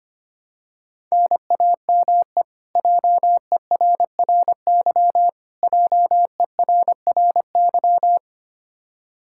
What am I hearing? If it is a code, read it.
Morse code, heard as a keyed tone, about 25 wpm, 700 Hz. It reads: NAME JERRY JERRY
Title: Practice QSO - Exchange 1 - Beginning - 25wpm